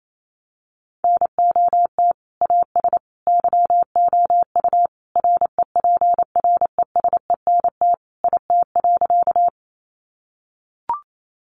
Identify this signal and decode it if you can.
Morse code, heard as a keyed tone, about 28 wpm, 700 Hz. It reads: NOT AS YOU REPRESENT IT.
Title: Sentences from Top 1200 Words 28wpm